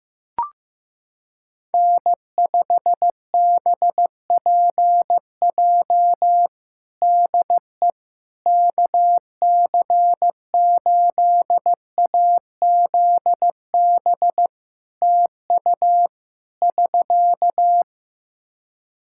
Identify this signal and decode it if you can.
Morse code, heard as a keyed tone, about 15 wpm, 700 Hz. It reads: N5BPJ DE KC8AZB TU <SK>
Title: K1USN SST QSO Practice 15wpm